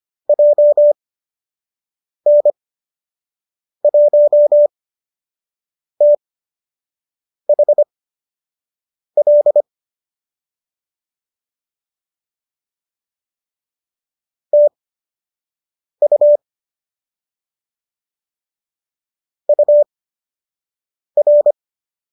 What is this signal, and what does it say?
Morse code, heard as a keyed tone, about 25 wpm, spaced as for 6 wpm, 590 Hz. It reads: JN1THL TU UR
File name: qso003 06